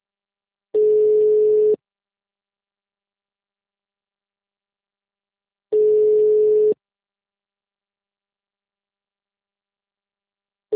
Ringback tone (Europe, 425 Hz).
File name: out- -2006- - - .10